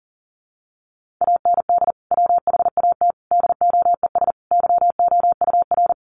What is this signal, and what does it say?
Morse code, heard as a keyed tone, about 40 wpm, 700 Hz. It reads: AND WHAT DOES YOUR
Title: Sentences from Top 1200 Words 40wpm